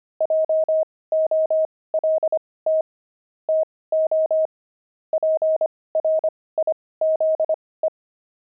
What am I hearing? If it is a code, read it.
Morse code, heard as a keyed tone, about 25 wpm, spaced as for 18 wpm, 630 Hz. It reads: JOLT TO PRIZE
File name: pangram-fs-18-cs-25-jolt